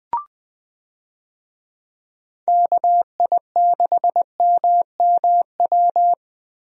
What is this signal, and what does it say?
Morse code, heard as a keyed tone, about 21 wpm, 700 Hz. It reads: KI6MMW
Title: USA Call Signs - 6 Characters - Repeated 2x - 20wpm